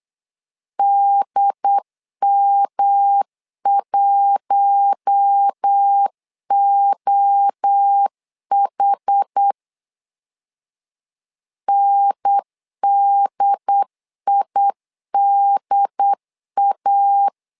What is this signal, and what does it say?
Morse code, heard as a keyed tone, about 8 wpm, 800 Hz. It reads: DM1OH NDIDA
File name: LEZIONI / 4 Q1 8wpm 3035